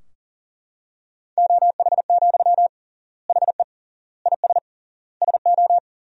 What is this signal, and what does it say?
Morse code, heard as a keyed tone, about 40 wpm, 700 Hz. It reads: OH, HE IS SO
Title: 3x Word-Spacing - Sentences from Top 500 Words - 40wpm